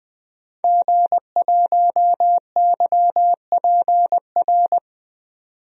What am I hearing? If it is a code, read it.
Morse code, heard as a keyed tone, about 20 wpm, 700 Hz. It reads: G1YPR
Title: DXCC-100 Call Signs - 5 Characters - 20wpm